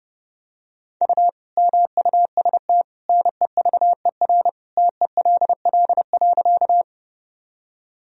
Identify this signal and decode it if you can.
Morse code, heard as a keyed tone, about 30 wpm, 700 Hz. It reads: U MUST NEVER TELL.
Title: Sentences from Top 300 Words and Abbreviations 30wpm